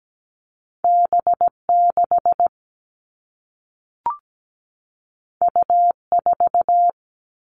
Morse code, heard as a keyed tone, about 17 wpm, 700 Hz. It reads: B6 U4
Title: All Cut Numbers - 2-Digit 17wpm